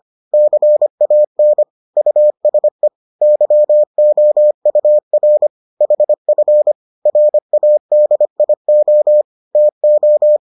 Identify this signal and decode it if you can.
Morse code, heard as a keyed tone, about 25 wpm, 600 Hz. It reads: CAN USE YOUR HF RADIO TO